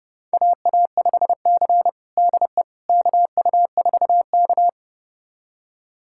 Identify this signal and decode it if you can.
Morse code, heard as a keyed tone, about 30 wpm, 700 Hz. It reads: AA5C DE KU4K
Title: USA Callsign DE Callsign - 4 Characters - 30wpm